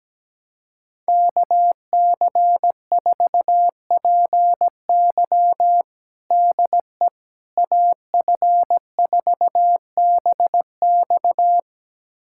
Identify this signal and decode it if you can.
Morse code, heard as a keyed tone, about 17 wpm, 700 Hz. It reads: KC4PY DE AF4BX
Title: USA Callsign DE Callsign - 5 Characters - 17wpm